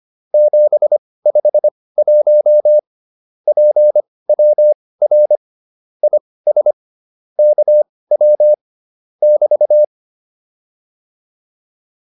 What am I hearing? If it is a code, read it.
Morse code, heard as a keyed tone, about 25 wpm, spaced as for 18 wpm, 600 Hz. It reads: 751 PWR IS KW <BT>